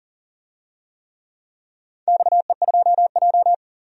Morse code, heard as a keyed tone, about 40 wpm, 700 Hz. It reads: XE2J
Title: DXCC-100 Call Signs - 4 Characters - 40wpm